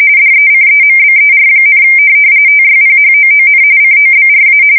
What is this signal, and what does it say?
RTTY, about 45.45 baud. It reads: DECEMBER, TYPHOON COBRA
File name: fsk170 / USS-Massachusetts